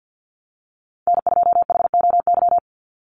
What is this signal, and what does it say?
Morse code, heard as a keyed tone, about 50 wpm, 700 Hz. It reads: N2HOY
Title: USA Call Signs 50wpm